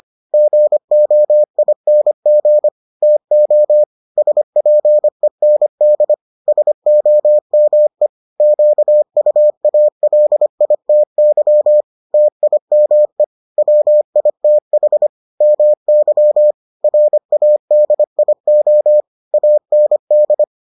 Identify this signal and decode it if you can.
Morse code, heard as a keyed tone, about 25 wpm, 600 Hz. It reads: GOING TO SPEND SOME QUALITY TIME WITH MY RADIO AND